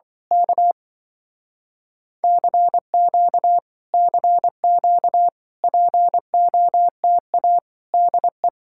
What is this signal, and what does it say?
Morse code, heard as a keyed tone, about 24 wpm, 700 Hz. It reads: K CQ CQ POTA DE
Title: Speed-Racing Common QSO Phrases Encore - 20wpm